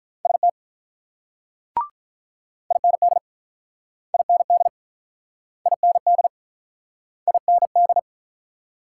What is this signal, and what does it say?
Morse code, heard as a keyed tone, about 47 wpm, 700 Hz. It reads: ST IND IND IND IND